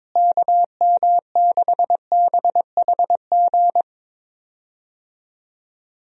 Morse code, heard as a keyed tone, about 22 wpm, 700 Hz. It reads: KM6BHG
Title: USA Call Signs - 6 Characters - Repeated 3x - 22wpm